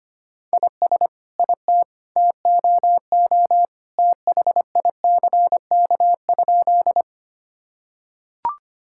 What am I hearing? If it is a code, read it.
Morse code, heard as a keyed tone, about 25 wpm, 700 Hz. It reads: IS IT TOO THICK?
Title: Sentences from Top 1600 Words 25wpm